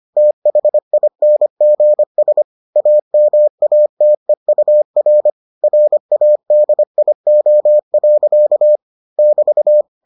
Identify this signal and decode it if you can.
Morse code, heard as a keyed tone, about 25 wpm, 600 Hz. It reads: THINGS AMATEUR RADIO. <BT>